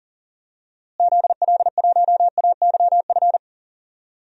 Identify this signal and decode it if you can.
Morse code, heard as a keyed tone, about 40 wpm, 700 Hz. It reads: ZL1AYF
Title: DXCC-100 Call Signs - 6 Characters - Repeated 3x - 40wpm